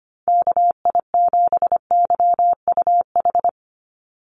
Morse code, heard as a keyed tone, about 25 wpm, 700 Hz. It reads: KI7YUH
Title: USA Call Signs 25wpm